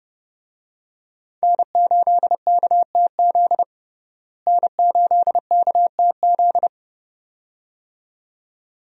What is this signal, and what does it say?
Morse code, heard as a keyed tone, about 30 wpm, 700 Hz. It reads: N8KTZ N8KTZ
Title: USA Call Signs - 5 Characters - Repeated 2x - 30wpm